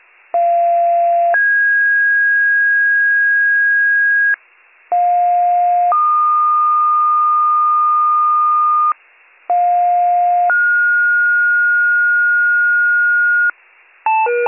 Two-tone page, heard as a one-second tone, then a three-second tone, about 690 then 1650 Hz, about 710 then 1220 Hz, about 710 then 1430 Hz, with noise. It is heard from 0.3 to 13.5 s.